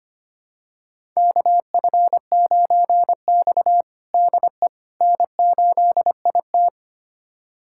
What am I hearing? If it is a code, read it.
Morse code, heard as a keyed tone, about 25 wpm, 700 Hz. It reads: KF9X DE N8IT
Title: USA Callsign DE Callsign - 4 Characters - 25wpm